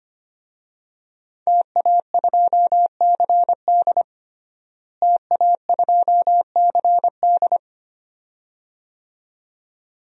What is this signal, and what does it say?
Morse code, heard as a keyed tone, about 25 wpm, 700 Hz. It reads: TA2CD TA2CD
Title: DXCC-100 Call Signs - 5 Characters - Repeated 2x - 25wpm